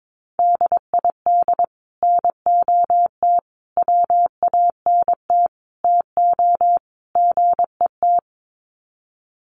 Morse code, heard as a keyed tone, about 22 wpm, 700 Hz. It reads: DID NOT WANT TO GET
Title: Sets of 5 Words Encore2 22wpm